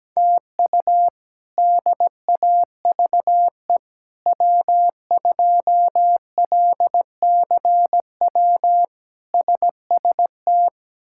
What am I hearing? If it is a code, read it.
Morse code, heard as a keyed tone, about 17 wpm, 700 Hz. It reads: TU DAVE W2LCW SST
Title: K1USN SST QSO Practice - 17wpm